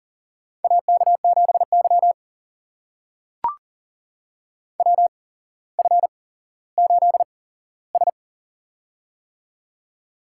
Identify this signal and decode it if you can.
Morse code, heard as a keyed tone, about 40 wpm, spaced as for 17 wpm, 700 Hz. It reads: AK7Y WF8S
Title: USA Call Signs - 4 Characters - 8x Character Spacing - 40wpm